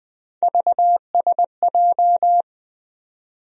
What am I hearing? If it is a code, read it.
Morse code, heard as a keyed tone, about 20 wpm, 700 Hz. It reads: VSJ
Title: USA Call Sign Postfixes - 3 Letters 20wpm